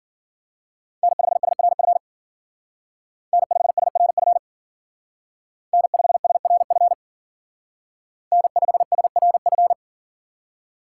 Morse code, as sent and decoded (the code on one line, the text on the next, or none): -. ..... ... .-. ..-. / -. ..... ... .-. ..-. / -. ..... ... .-. ..-. / -. ..... ... .-. ..-.
N5SRF N5SRF N5SRF N5SRF